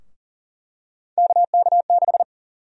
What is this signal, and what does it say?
Morse code, heard as a keyed tone, about 40 wpm, 700 Hz. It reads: KK6
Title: USA Call Sign Prefixes - 40wpm